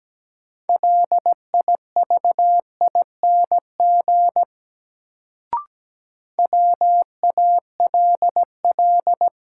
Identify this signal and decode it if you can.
Morse code, heard as a keyed tone, about 17 wpm, 700 Hz. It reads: LIVING WALL
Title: Sentences from Top 1100 Words - New Word Drill Alt - 17wpm